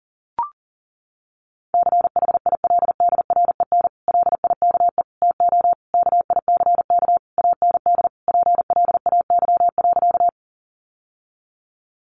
Morse code, heard as a keyed tone, about 40 wpm, 700 Hz. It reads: CHILDREN LIKE TO KICK AND PLAY.
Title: Sentences from Top 1300 Words 40wpm